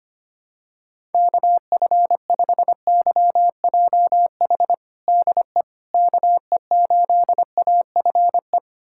Morse code, heard as a keyed tone, about 25 wpm, 700 Hz. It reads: KF5YJH DE KE8AFE
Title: USA Callsign DE Callsign - 6 Characters - 25wpm